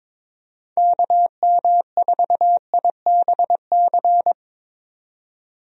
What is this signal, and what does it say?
Morse code, heard as a keyed tone, about 21 wpm, 700 Hz. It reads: KM4IBC
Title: USA Call Signs - 6 Characters - 22wpm